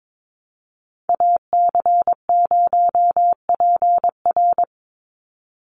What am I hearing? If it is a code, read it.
Morse code, heard as a keyed tone, about 22 wpm, 700 Hz. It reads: AC0PR
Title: NA SOTA Activator Callsigns 22wpm